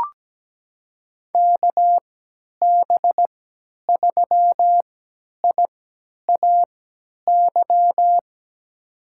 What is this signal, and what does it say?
Morse code, heard as a keyed tone, about 17 wpm, spaced as for 10 wpm, 700 Hz. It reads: KB3IAY
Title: USA Call Signs - 6 Characters - 3x Character Spacing - 17wpm